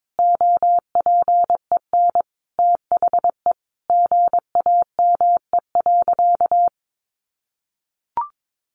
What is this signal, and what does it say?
Morse code, heard as a keyed tone, about 22 wpm, 700 Hz. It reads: OPEN THE GAME.
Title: Sentences from Top 1500 Words 22wpm